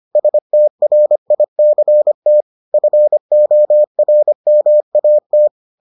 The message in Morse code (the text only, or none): STRICT FORMAT